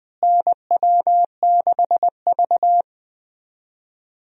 Morse code, heard as a keyed tone, about 20 wpm, 700 Hz. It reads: NW6V
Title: USA Call Signs - 4 Characters - Repeated 3x - 20wpm